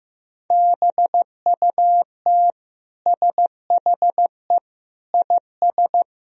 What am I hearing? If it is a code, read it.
Morse code, heard as a keyed tone, about 15 wpm, 700 Hz. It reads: BUT SHE IS
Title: Sentences from Top 400 Words and Abbreviations 15wpm